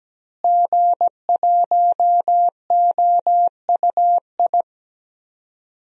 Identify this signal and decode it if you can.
Morse code, heard as a keyed tone, about 17 wpm, 700 Hz. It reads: G1OUI
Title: DXCC-100 Call Signs - 5 Characters - Repeated 2x - 17wpm